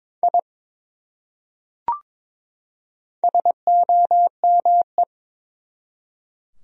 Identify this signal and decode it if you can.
Morse code, heard as a keyed tone, about 22 wpm, 700 Hz. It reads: I SOME
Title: Top 100 Words 22wpm